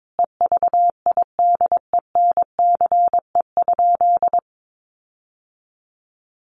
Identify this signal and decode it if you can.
Morse code, heard as a keyed tone, about 22 wpm, 700 Hz. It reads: EVIDENCE?
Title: Sentences from Top 1300 Words 22wpm